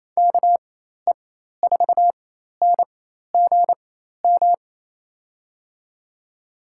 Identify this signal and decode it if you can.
Morse code, heard as a keyed tone, about 27 wpm, spaced as for 13 wpm, 700 Hz. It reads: KE4NGM
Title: USA Call Signs - 6 Characters - 4x Character Spacing - 28wpm